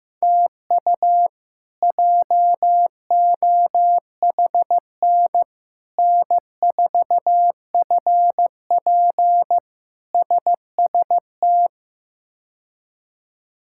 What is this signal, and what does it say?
Morse code, heard as a keyed tone, about 15 wpm, 700 Hz. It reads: TU JOHN N4FP SST
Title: K1USN SST QSO Practice - 15wpm